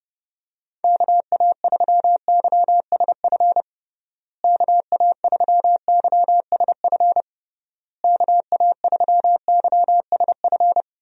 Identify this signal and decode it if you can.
Morse code, heard as a keyed tone, about 32 wpm, 700 Hz. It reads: KA3YSF KA3YSF KA3YSF
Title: USA Call Signs - 6 Characters - Repeated 3x - 30wpm